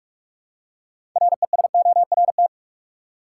Morse code, heard as a keyed tone, about 46 wpm, 700 Hz. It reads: RESORT
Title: Six Letter Words - 45wpm